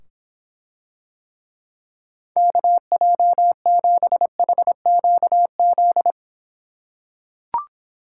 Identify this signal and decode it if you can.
Morse code, heard as a keyed tone, about 26 wpm, 700 Hz. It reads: KJ7HQZ